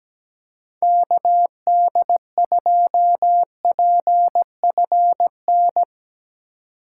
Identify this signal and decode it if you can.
Morse code, heard as a keyed tone, about 17 wpm, 700 Hz. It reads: KD2PFN
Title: USA Call Signs - 4x Character Spacing - 17wpm